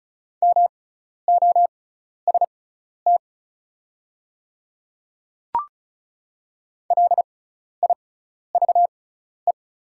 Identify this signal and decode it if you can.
Morse code, heard as a keyed tone, about 36 wpm, spaced as for 12 wpm, 700 Hz. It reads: MOST LIVE
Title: 6x Character Spacing - Top 200 Words - 35wpm